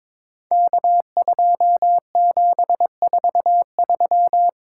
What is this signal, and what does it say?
Morse code, heard as a keyed tone, about 22 wpm, 700 Hz. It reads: K2743
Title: Code Groups - 5 Letter:Numbers - 22wpm